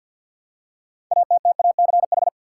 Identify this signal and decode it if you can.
Morse code, heard as a keyed tone, about 47 wpm, 700 Hz. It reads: ATTACH
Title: Six Letter Words - 50wpm